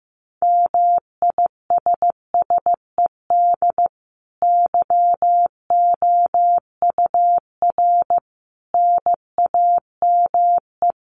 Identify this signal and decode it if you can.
Morse code, heard as a keyed tone, about 15 wpm, 700 Hz. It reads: MISSED YOUR NAME
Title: Common QSO Phrases 15wpm